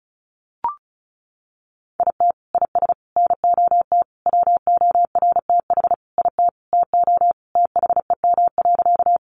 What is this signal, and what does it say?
Morse code, heard as a keyed tone, about 34 wpm, 700 Hz. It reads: IT IS NOT WORTH IT TO THEM.
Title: Sentences from Top 800 Words 35wpm